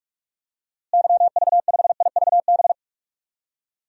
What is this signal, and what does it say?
Morse code, heard as a keyed tone, about 41 wpm, 700 Hz. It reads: YV5IVB
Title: DXCC-100 Call Signs - 6 Characters - Repeated 3x - 45wpm